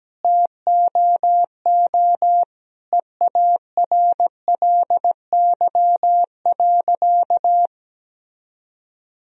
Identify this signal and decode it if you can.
Morse code, heard as a keyed tone, about 17 wpm, 700 Hz. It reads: TOO EARLY.
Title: Sentences from Top 1000 Words - Review Part 2 - 17wpm